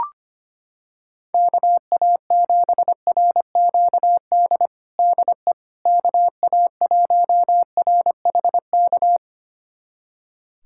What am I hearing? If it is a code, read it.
Morse code, heard as a keyed tone, about 25 wpm, 700 Hz. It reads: KA7RQD DE KA1RHK
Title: USA Callsign DE Callsign - 6 Characters - 25wpm